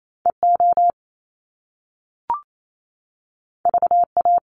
Morse code, heard as a keyed tone, about 28 wpm, 700 Hz. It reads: EO VA
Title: Top Two Letter Combinations in Words 28wpm